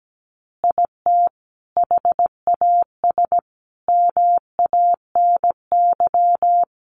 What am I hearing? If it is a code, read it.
Morse code, heard as a keyed tone, about 17 wpm, 700 Hz. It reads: IT HAS MANY
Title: Additional Easy Phrases 17wpm